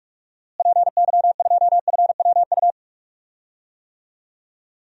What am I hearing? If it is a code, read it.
Morse code, heard as a keyed tone, about 46 wpm, 700 Hz. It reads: PY2FWU
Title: DXCC-100 Call Signs - 6 Characters - Repeated 2x - 45wpm